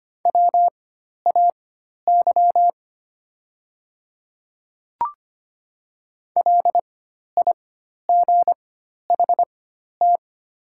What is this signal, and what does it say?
Morse code, heard as a keyed tone, about 25 wpm, spaced as for 12 wpm, 690 Hz. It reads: WAY LIGHT
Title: 4x Character Spacing - Top 200 Words - 25wpm